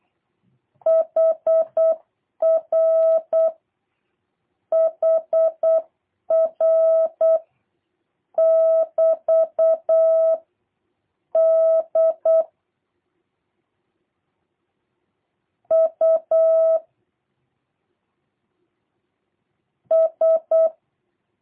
Morse code, heard as a keyed tone, about 8 wpm, 650 Hz. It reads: HR HR <BT> D U S
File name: jh8cbh / 2023cw / review1-3